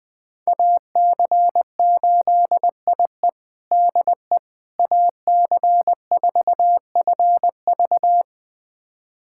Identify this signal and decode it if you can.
Morse code, heard as a keyed tone, about 20 wpm, 700 Hz. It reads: AC8IE DE AC4FV
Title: USA Callsign DE Callsign - 5 Characters - 20wpm